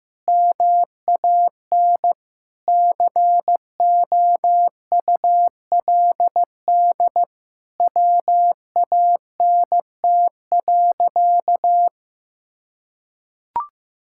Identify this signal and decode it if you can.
Morse code, heard as a keyed tone, about 15 wpm, 700 Hz. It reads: MAN COULD WANT.
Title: Sentences from Top 1000 Words - Review Part 5 - 15wpm